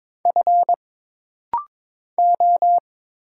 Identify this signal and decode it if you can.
Morse code, heard as a keyed tone, about 22 wpm, 700 Hz. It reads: F O
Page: This rapid-fire timing is geared to push your ICR (Instant Character Recognition) skills.